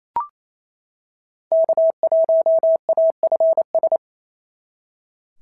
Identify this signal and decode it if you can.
Morse code, heard as a keyed tone, about 26 wpm, 650 Hz. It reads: K1AFS
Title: SOTA QSO Practice 28wpm